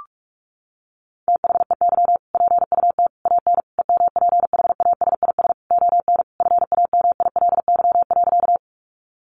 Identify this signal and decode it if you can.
Morse code, heard as a keyed tone, about 44 wpm, 700 Hz. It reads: THEY PUT AN EMPHASIS ON FAMILY.
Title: Sentences from Top 1400 Words 45wpm